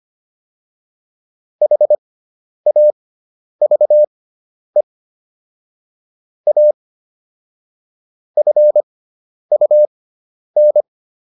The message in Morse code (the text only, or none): HAVE A FUN